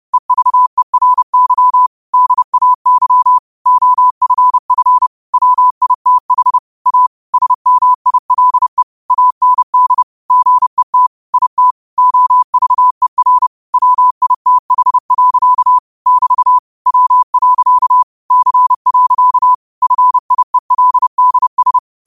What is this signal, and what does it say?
Morse code, heard as a keyed tone, about 30 wpm, 1000 Hz. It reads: EVERY DAY OFF WITH A SMILE AND GET IT OVER WITH. <BT> W. C. FIELDS